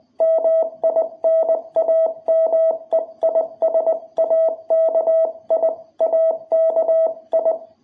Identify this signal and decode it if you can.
Morse code, heard as a keyed tone, about 20 wpm, 660 Hz. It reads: MI NAME IS AKI AKI
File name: jh8cbh / 2021cw